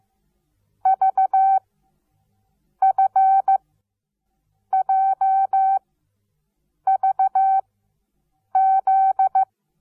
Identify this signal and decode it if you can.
Morse code, heard as a keyed tone, about 15 wpm, spaced as for 6 wpm, 770 Hz. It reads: VFJVZ
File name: Onlinelehrgang / morsen / DJ4UF Morsen17